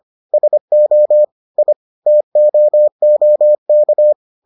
Morse code, heard as a keyed tone, about 25 wpm, 600 Hz. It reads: SO I TOOK